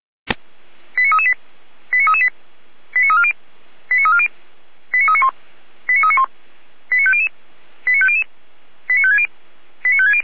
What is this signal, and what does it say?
Five-tone selective call (ZVEI), heard as quick runs of five tones, tones of about 70 ms. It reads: 89338 89338 89340 89340 89391 89391 89500 89500 89560 89560
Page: audio.f85360.feuerwehr-orlamuende.org